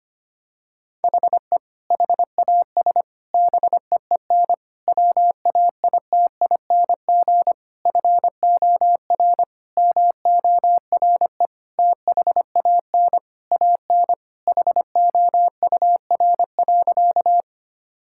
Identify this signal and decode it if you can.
Morse code, heard as a keyed tone, about 25 wpm, 700 Hz. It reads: HE HAS BEEN WAITING FOR MORE THAN AN HOUR.
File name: practice / 25wpm / Sentences with 3-Word Phrases - 25wpm